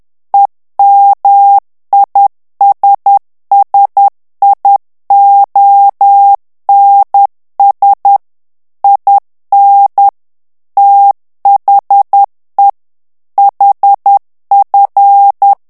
Morse code, heard as a keyed tone, about 10 wpm, 800 Hz. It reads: EMISSIONS IN THE HF
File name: ANGLAIS texte clair 3 vitesse 10 mots minute